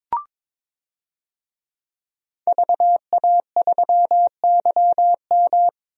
Morse code, heard as a keyed tone, about 22 wpm, 700 Hz. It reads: VA3YM
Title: DXCC-100 Call Signs - 5 Characters - 22wpm